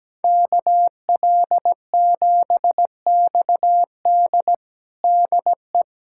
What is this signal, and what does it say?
Morse code, heard as a keyed tone, about 17 wpm, 700 Hz. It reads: KL7XD DE
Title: USA Callsign DE Callsign - 5 Characters - 17wpm